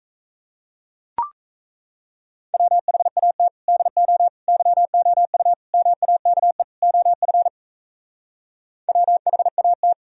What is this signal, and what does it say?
Morse code, heard as a keyed tone, about 36 wpm, 700 Hz. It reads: WHAT DO YOU MAKE OF WHAT